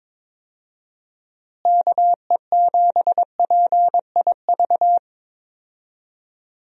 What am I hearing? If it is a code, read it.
Morse code, heard as a keyed tone, about 22 wpm, 700 Hz. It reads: KE7PIV
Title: USA Call Signs - 6 Characters - Repeated 3x - 22wpm